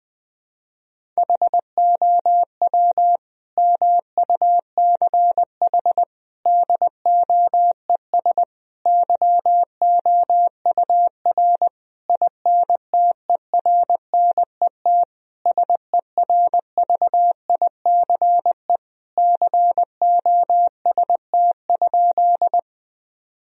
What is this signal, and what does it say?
Morse code, heard as a keyed tone, about 20 wpm, 700 Hz. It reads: HOW MUCH DOES YOUR INTERNET SERVICE COST?
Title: Sentences from Top 1000 Words - Review Part 4 - 20wpm